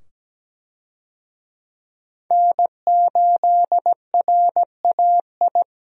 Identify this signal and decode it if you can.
Morse code, heard as a keyed tone, about 17 wpm, 700 Hz. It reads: N8RAI
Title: USA Call Signs - 5 Characters - 17wpm